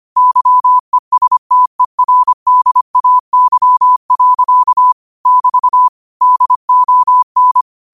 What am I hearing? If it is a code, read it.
Morse code, heard as a keyed tone, about 25 wpm, 1000 Hz. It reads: YESTERDAY. <BT> DON
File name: qotdfiles / 25wpm